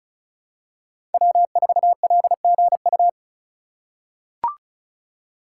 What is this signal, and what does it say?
Morse code, heard as a keyed tone, about 36 wpm, 700 Hz. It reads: W4LGU
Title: Code Groups - 5 Letter:Numbers - 35wpm